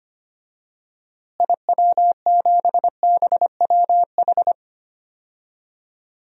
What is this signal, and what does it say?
Morse code, heard as a keyed tone, about 25 wpm, 700 Hz. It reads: IW7BWH
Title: DXCC-100 Call Signs - 6 Characters - Repeated 2x - 25wpm